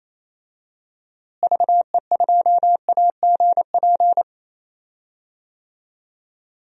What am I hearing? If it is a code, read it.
Morse code, heard as a keyed tone, about 28 wpm, 700 Hz. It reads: VE2AGP
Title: DXCC-100 Call Signs - 6 Characters - Repeated 2x - 28wpm